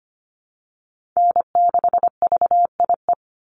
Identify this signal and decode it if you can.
Morse code, heard as a keyed tone, about 25 wpm, 700 Hz. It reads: N6VIE
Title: USA Call Signs 25wpm